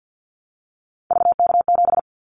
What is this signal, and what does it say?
Morse code, heard as a keyed tone, about 50 wpm, 700 Hz. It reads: VK7
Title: Callsign Prefixes for Top 100 DXCC - 50wpm